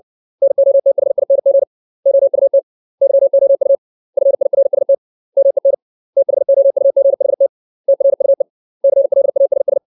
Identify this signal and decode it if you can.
Morse code, heard as a keyed tone, about 60 wpm, 550 Hz. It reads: NOTHING OUT YOU FIRST MEN THOUGHT TRUE CLASS
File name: 632 words / 632 Words 60WPM